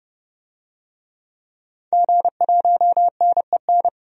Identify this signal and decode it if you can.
Morse code, heard as a keyed tone, about 32 wpm, 690 Hz. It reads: G1NEN
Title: DXCC-100 Call Signs - 5 Characters - 30wpm